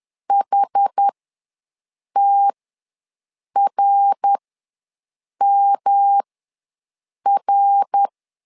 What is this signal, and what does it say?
Morse code, heard as a keyed tone, about 11 wpm, spaced as for 6 wpm, 800 Hz. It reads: HTRMR
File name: LEZIONI / 3 MWRO / 3 MWR ws 10wpm 2151